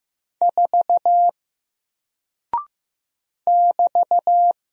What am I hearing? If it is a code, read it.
Morse code, heard as a keyed tone, about 15 wpm, 700 Hz. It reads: 4 <BT>
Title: Single Character (Letter:Number:Punctuation:Prosign) 15wpm